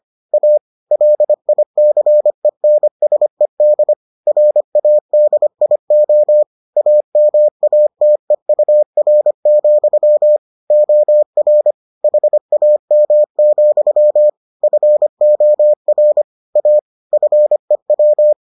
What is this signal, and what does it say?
Morse code, heard as a keyed tone, about 25 wpm, 600 Hz. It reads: A LICENSED RADIO AMATEUR, OR HAM, FOR A FEW